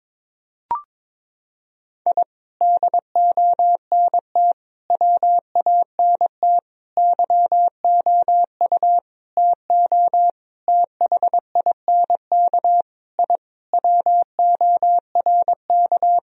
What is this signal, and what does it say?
Morse code, heard as a keyed tone, about 22 wpm, 700 Hz. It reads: I DONT WANT YOU TO THINK I WORK
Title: Sentences from Top 1000 Words - Review Part 3 - 22wpm